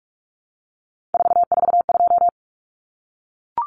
Morse code, heard as a keyed tone, about 44 wpm, 700 Hz. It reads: 442
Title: Numbers - 3-Digit 45wpm